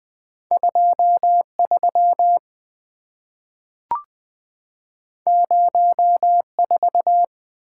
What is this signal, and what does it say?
Morse code, heard as a keyed tone, about 20 wpm, 700 Hz. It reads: 23 04
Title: Two Letter-Number Combinations - 20wpm